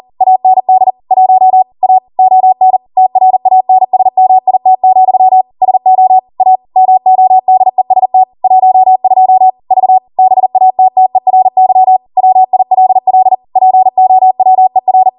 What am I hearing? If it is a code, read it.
Morse code, heard as a keyed tone, about 40 wpm, 750 Hz. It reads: AND 1 A ON TRANSMIT, SO A MODEST 12 V BATTERY WILL POWER
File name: Morse / 40WPM